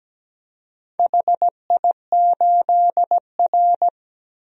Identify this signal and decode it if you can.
Morse code, heard as a keyed tone, about 17 wpm, 700 Hz. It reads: HI8R
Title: DXCC-100 Call Signs - 4 Characters - Repeated 2x - 17wpm